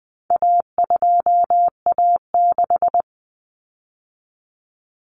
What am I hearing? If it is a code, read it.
Morse code, heard as a keyed tone, about 20 wpm, 700 Hz. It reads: A2A6
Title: Common Cut Numbers - 4-Digit 20wpm